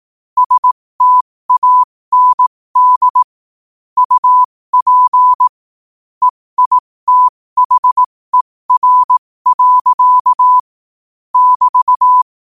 Morse code, heard as a keyed tone, about 18 wpm, spaced as for 15 wpm, 1000 Hz. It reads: STAND UP EITHER. <BT>
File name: qotdfiles / 15wpm